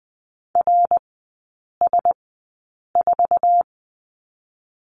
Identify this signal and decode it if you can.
Morse code, heard as a keyed tone, about 20 wpm, spaced as for 8 wpm, 700 Hz. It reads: RS4
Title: 087 Lesson 15e - Number 5 - Three Characters 20wpm